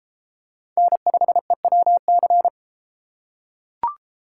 Morse code, heard as a keyed tone, about 30 wpm, 700 Hz. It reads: N5EWC